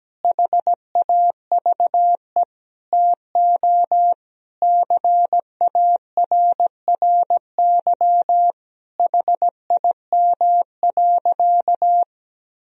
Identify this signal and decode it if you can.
Morse code, heard as a keyed tone, about 17 wpm, 700 Hz. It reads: HAVE TO CARRY HIM.
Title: Sentences from Top 1000 Words - Review Part 1 - 17wpm